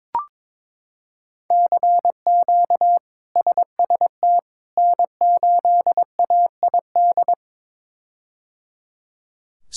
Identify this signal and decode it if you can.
Morse code, heard as a keyed tone, about 22 wpm, 700 Hz. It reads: CQ SST N8AID
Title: K1USN SST QSO Practice - 22wpm